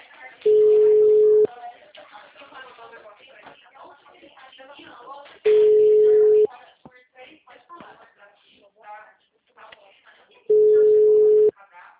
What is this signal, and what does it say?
Ringback tone (Europe, 425 Hz), with speech.